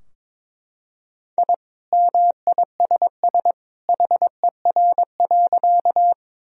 Morse code, heard as a keyed tone, about 22 wpm, 700 Hz. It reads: I MISS HER.
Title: Sentences from Top 1000 Words - Review Part 3 - 22wpm